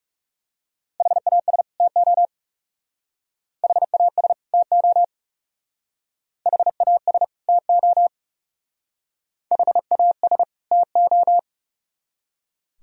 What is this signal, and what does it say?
Morse code, heard as a keyed tone, about 37 wpm, 700 Hz. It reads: HAS TO HAS TO HAS TO HAS TO
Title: Speed-Racing - Sets of 2 Words 30wpm